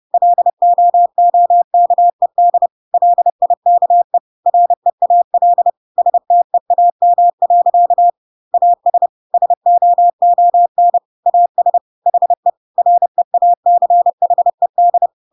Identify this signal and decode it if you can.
Morse code, heard as a keyed tone, about 30 wpm, 700 Hz. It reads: LOOKED LIKE REAL STEAM. AS SOON AS HE REACHED
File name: Pinocchio / Pinocchio ch3 30 30 ik1hge